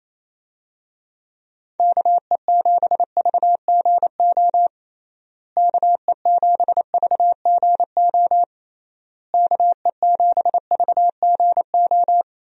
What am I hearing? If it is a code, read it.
Morse code, heard as a keyed tone, about 28 wpm, 700 Hz. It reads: KE7VGO KE7VGO KE7VGO
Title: USA Call Signs - 6 Characters - Repeated 3x - 28wpm